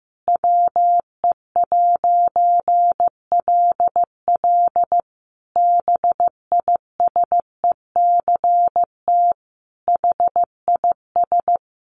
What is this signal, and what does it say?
Morse code, heard as a keyed tone, about 15 wpm, 700 Hz. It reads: WE'LL BISECT HIS
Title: A Sound Of Thunder 15wpm